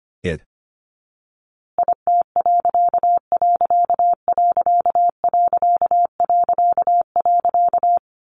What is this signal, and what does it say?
Morse code, heard as a keyed tone, about 25 wpm, 700 Hz. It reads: IT......
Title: A Sound Of Thunder 25wpm